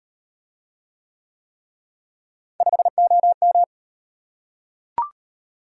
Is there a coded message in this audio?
Yes: Morse code from 2.6 to 3.6 s, heard as a keyed tone, about 38 wpm, 700 Hz. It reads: 5OM